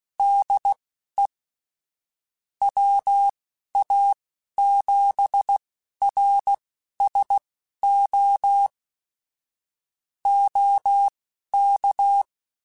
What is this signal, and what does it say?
Morse code, heard as a keyed tone, about 16 wpm, spaced as for 11 wpm, 800 Hz. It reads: DE WA7RSO OK
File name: (spaced) / QSOs Spaced-mp3s / WD0FBT 1 @15 sp